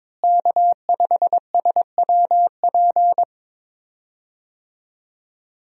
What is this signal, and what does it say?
Morse code, heard as a keyed tone, about 22 wpm, 700 Hz. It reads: K5SWP
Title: USA Call Signs - 5 Characters - Repeated 2x - 22wpm